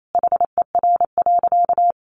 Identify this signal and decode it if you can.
Morse code, heard as a keyed tone, about 28 wpm, 700 Hz. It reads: HER.
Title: Sentences from Top 1500 Words 28wpm